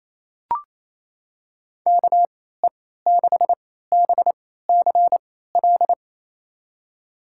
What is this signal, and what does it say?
Morse code, heard as a keyed tone, about 28 wpm, spaced as for 16 wpm, 700 Hz. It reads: KE6BCL
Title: USA Call Signs - 3x Character Spacing - 28wpm